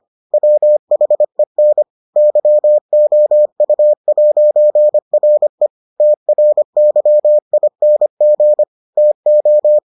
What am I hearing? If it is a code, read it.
Morse code, heard as a keyed tone, about 25 wpm, 600 Hz. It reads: WHEN YOU'RE TRYING TO